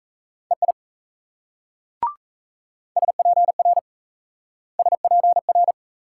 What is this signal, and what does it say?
Morse code, heard as a keyed tone, about 34 wpm, 700 Hz. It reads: EI SPR SPR
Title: Speed-Racing Multi-letter Phonemes - 28wpm